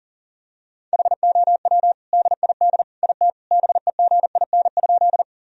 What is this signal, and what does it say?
Morse code, heard as a keyed tone, about 40 wpm, 700 Hz. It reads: HOW DID IT BEGIN?
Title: Sentences from Top 1000 Words - Review Part 1 - 40wpm